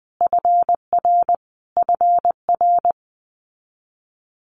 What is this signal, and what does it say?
Morse code, heard as a keyed tone, about 20 wpm, 700 Hz. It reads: FR FR
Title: Multi-letter Phonemes Repeated 3x 20wpm